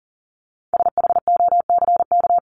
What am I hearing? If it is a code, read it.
Morse code, heard as a keyed tone, about 40 wpm, 700 Hz. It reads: SHOCK
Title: Top 1751-2000 Words 40wpm